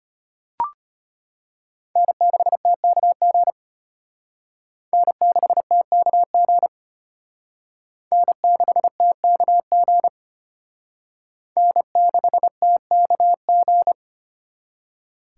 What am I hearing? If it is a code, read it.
Morse code, heard as a keyed tone, about 32 wpm, 700 Hz. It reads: N6TKG N6TKG N6TKG N6TKG